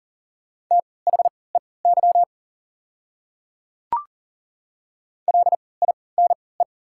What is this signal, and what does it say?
Morse code, heard as a keyed tone, about 40 wpm, spaced as for 23 wpm, 690 Hz. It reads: THEY LINE
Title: 3x Character Spacing - Top 200 Words - 40wpm